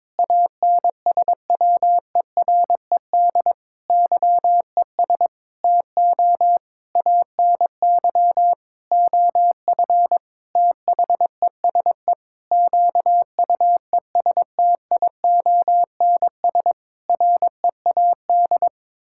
Morse code, heard as a keyed tone, about 22 wpm, 700 Hz. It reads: ANSWERED YES TO ANY OF THESE QUESTIONS READ
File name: Sentences with 9-Word Phrases - 22wpm